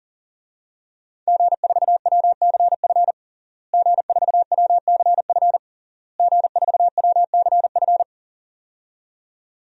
Morse code, heard as a keyed tone, about 40 wpm, 700 Hz. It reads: G4WCF G4WCF G4WCF
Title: DXCC-100 Call Signs - 5 Characters - Repeated 3x - 40wpm